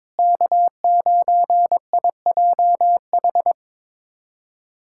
Morse code, heard as a keyed tone, about 22 wpm, 700 Hz. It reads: K9IJH
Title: USA Callsign DE Callsign - 5 Characters - 22wpm